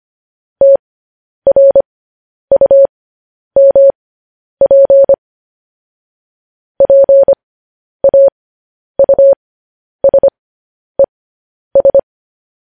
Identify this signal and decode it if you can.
Morse code, heard as a keyed tone, about 25 wpm, spaced as for 10 wpm, 550 Hz. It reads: TRUMP PAUSES
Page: news headlines in morse code at 10 wpm